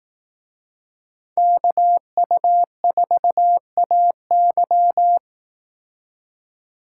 Morse code, heard as a keyed tone, about 18 wpm, 700 Hz. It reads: KU4AY
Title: Speed Racing - USA 2x2 Call Signs 15wpm